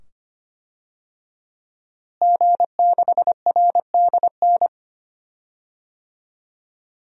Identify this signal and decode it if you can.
Morse code, heard as a keyed tone, about 25 wpm, 700 Hz. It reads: G6RDN
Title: DXCC-100 Call Signs - 5 Characters - Repeated 2x - 25wpm